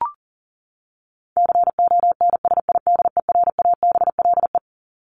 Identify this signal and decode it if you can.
Morse code, heard as a keyed tone, about 40 wpm, 700 Hz. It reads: CONSIDERABLE
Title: LY Suffix - Top 100 Words with Base Words 40wpm